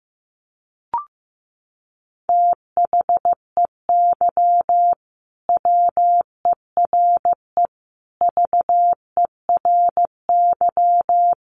Morse code, heard as a keyed tone, about 15 wpm, 700 Hz. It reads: THEY WERE VERY
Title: Sentences from Top 1100 Words 15wpm